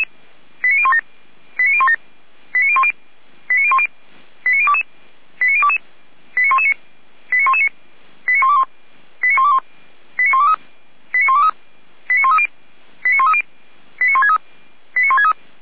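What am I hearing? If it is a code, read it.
Five-tone selective call (ZVEI), heard as quick runs of five tones, tones of about 70 ms. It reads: 89017 89017 89010 89010 89022 89022 89119 89119 89121 89121 89123 89123 89130 89130 89163 89163
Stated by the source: ZVEI five-tone selective call received on an SDR.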